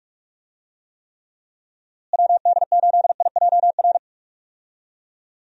Morse code, heard as a keyed tone, about 42 wpm, 700 Hz. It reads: WD8IJR
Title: USA Call Signs - 6 Characters - Repeated 3x - 45wpm